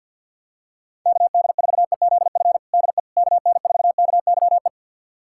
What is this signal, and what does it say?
Morse code, heard as a keyed tone, about 52 wpm, 700 Hz. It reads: KD4EZF DE KN4KYE
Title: USA Callsign DE Callsign 50wpm